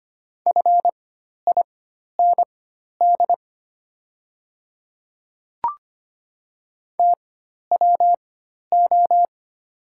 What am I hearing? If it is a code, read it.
Morse code, heard as a keyed tone, about 25 wpm, spaced as for 12 wpm, 690 Hz. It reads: FIND TWO
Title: 4x Character Spacing - Top 300 Words - 25wpm